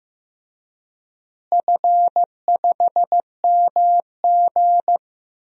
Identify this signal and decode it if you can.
Morse code, heard as a keyed tone, about 15 wpm, 700 Hz. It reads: F5MG
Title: DXCC-100 Call Signs - 4 Characters - Repeated 2x - 15wpm